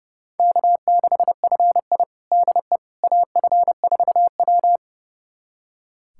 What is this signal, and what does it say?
Morse code, heard as a keyed tone, about 32 wpm, 700 Hz. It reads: K6FI DE AF4W
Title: USA Callsign DE Callsign - 4 Characters - 30wpm